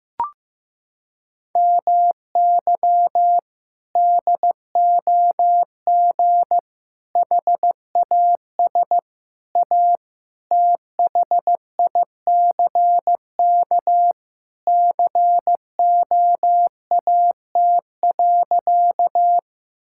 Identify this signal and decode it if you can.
Morse code, heard as a keyed tone, about 15 wpm, 700 Hz. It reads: MY DOG HAS A THICK COAT.
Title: Sentences from Top 1600 Words 15wpm